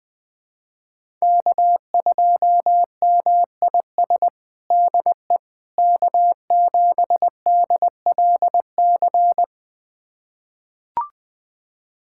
Morse code, heard as a keyed tone, about 21 wpm, 700 Hz. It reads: K2MIS DE K7DLC
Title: USA Callsign DE Callsign - 5 Characters - 20wpm